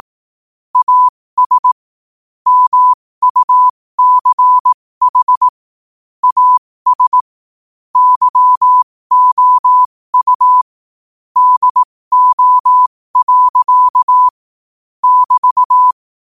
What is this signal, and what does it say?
Morse code, heard as a keyed tone, about 18 wpm, spaced as for 15 wpm, 1000 Hz. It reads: AS MUCH AS YOU DO. <BT>
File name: qotdfiles / 15wpm